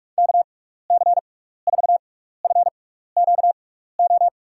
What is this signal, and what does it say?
Morse code, heard as a keyed tone, about 46 wpm, spaced as for 15 wpm, 700 Hz. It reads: KC4FQO
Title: USA Call Signs - 6x Character Spacing - 45wpm